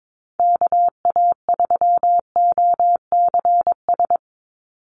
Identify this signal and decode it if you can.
Morse code, heard as a keyed tone, about 22 wpm, 700 Hz. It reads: KA3OCS
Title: Call Signs - 22wpm